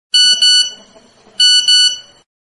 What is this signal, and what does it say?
There is no code.